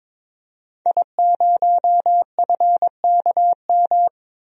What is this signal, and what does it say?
Morse code, heard as a keyed tone, about 22 wpm, 700 Hz. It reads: I0FKM
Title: DXCC-100 Call Signs - 5 Characters - Repeated 3x - 22wpm